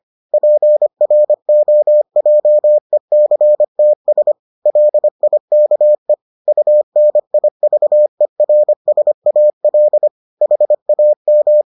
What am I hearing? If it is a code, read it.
Morse code, heard as a keyed tone, about 25 wpm, 600 Hz. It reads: PROJECTS LIKE UNIVERSAL HAM